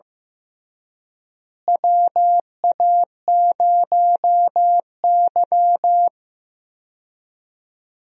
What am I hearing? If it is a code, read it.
Morse code, heard as a keyed tone, about 15 wpm, 700 Hz. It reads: WA0Y
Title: USA Call Signs - 4 Characters - Repeated 3x - 15wpm